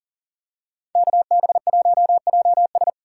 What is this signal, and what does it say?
Morse code, heard as a keyed tone, about 39 wpm, 700 Hz. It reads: KB1JS
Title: USA Call Signs - 5 Characters - 4x Character Spacing - 40wpm